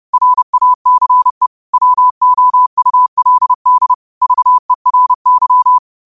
Morse code, heard as a keyed tone, about 30 wpm, 1000 Hz. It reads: RACE WOULD VERY
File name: qotdfiles / 30wpm